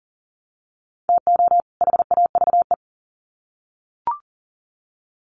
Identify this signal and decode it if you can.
Morse code, heard as a keyed tone, about 40 wpm, 700 Hz. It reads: TO HAVE
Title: Sets of 2 Words 40wpm